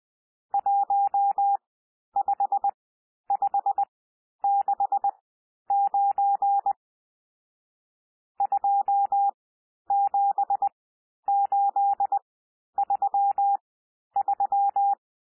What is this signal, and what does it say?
Morse code, heard as a keyed tone, about 20 wpm, spaced as for 11 wpm, 800 Hz. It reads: 15569 27833
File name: cwmp3 / N1 SOUND NUMBERS 20 wpm